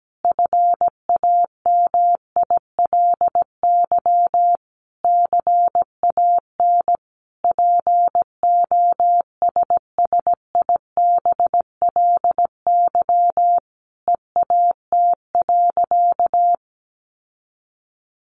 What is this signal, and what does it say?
Morse code, heard as a keyed tone, about 17 wpm, 700 Hz. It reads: FAMILY CAN POSSIBLY EAT.
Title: Sentences from Top 1300 Words 17wpm